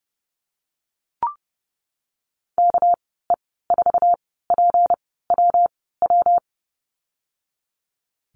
Morse code, heard as a keyed tone, about 31 wpm, spaced as for 17 wpm, 700 Hz. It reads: KE4PWW
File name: Lesson 37g - Letter x - Call Signs 30wpm